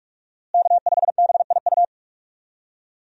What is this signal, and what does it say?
Morse code, heard as a keyed tone, about 42 wpm, 700 Hz. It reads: K5BIU
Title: USA Call Signs - 5 Characters - Repeated 2x - 45wpm